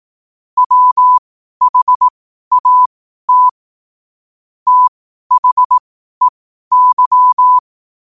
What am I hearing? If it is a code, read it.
Morse code, heard as a keyed tone, about 18 wpm, spaced as for 12 wpm, 1000 Hz. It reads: WHAT THEY
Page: 12-wpm-morse-code-quotes